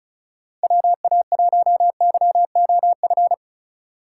Morse code, heard as a keyed tone, about 36 wpm, 700 Hz. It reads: WA1YOF
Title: USA Call Signs - 6 Characters - 3x Character Spacing - 35wpm